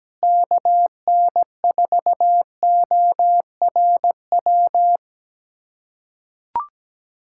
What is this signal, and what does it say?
Morse code, heard as a keyed tone, about 17 wpm, 700 Hz. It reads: KN4ORW
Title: USA Callsign DE Callsign - 6 Characters - 17wpm